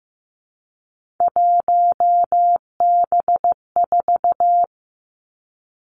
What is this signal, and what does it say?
Morse code, heard as a keyed tone, about 15 wpm, 700 Hz. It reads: 1B4
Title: Three Challenging Letter-Number Combinations 15wpm